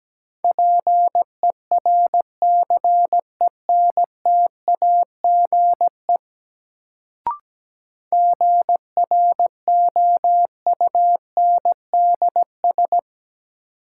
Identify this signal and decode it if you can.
Morse code, heard as a keyed tone, about 17 wpm, 700 Hz. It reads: PERCENTAGE GROUNDS
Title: Sentences from Top 1300 Words - New Word Drill - 17wpm